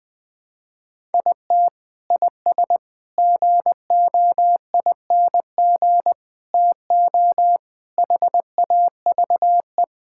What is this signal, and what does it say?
Morse code, heard as a keyed tone, about 20 wpm, 700 Hz. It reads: IT IS GOING TO HAVE
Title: Sentences with 5-Word Phrases - 20wpm